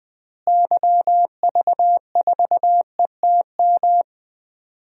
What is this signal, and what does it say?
Morse code, heard as a keyed tone, about 20 wpm, 700 Hz. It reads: YV4ETM
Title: DXCC-100 Call Signs - 6 Characters - Repeated 3x - 20wpm